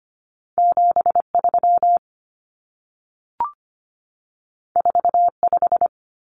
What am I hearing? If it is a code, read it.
Morse code, heard as a keyed tone, about 25 wpm, 700 Hz. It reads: 73 45
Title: Numbers - 2-Digit 25wpm